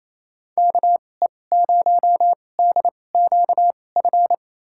Morse code, heard as a keyed tone, about 27 wpm, spaced as for 20 wpm, 700 Hz. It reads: KE0DQF
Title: USA Call Signs - 6 Characters - 2x Character Spacing - 28wpm